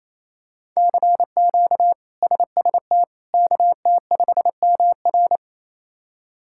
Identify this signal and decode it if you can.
Morse code, heard as a keyed tone, about 28 wpm, 700 Hz. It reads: CQ SST KT5MR
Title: K1USN SST QSO Practice - 28wpm